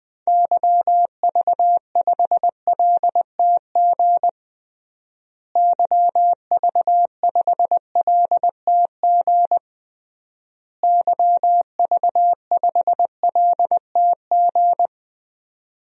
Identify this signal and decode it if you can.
Morse code, heard as a keyed tone, about 20 wpm, 700 Hz. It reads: YV5LTG YV5LTG YV5LTG
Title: DXCC-100 Call Signs - 6 Characters - Repeated 3x - 20wpm